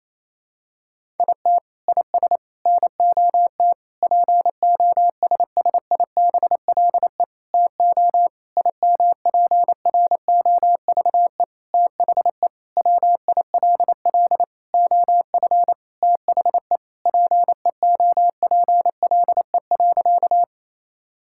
Morse code, heard as a keyed tone, about 28 wpm, 700 Hz. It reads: IT IS NOT POSSIBLE TO IMPROVE THE WILL OF THE PEOPLE.
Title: Sentences from Top 1000 Words - Review Part 5 - 28wpm